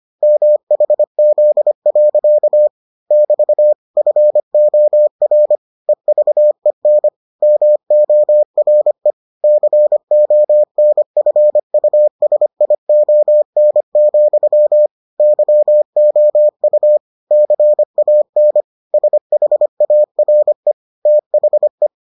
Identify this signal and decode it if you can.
Morse code, heard as a keyed tone, about 25 wpm, 600 Hz. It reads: MHZ. <BT> FOR EVEN MORE CONFUSION, YOU CAN SHARE THE